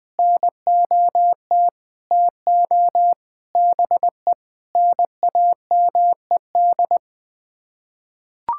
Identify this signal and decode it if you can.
Morse code, heard as a keyed tone, about 20 wpm, 700 Hz. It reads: NOT TO BE NAMED
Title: Sets of 6 Words - 20wpm